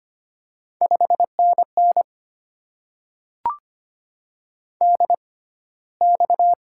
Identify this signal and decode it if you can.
Morse code, heard as a keyed tone, about 25 wpm, spaced as for 15 wpm, 700 Hz. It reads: 5NN DX
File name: Top 90 QSO Elements - 6x Character Spacing - 25wpm